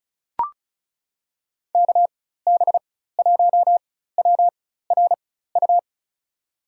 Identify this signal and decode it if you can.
Morse code, heard as a keyed tone, about 34 wpm, spaced as for 16 wpm, 690 Hz. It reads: KB1WRU
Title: USA Call Signs - 6 Characters - 4x Character Spacing - 35wpm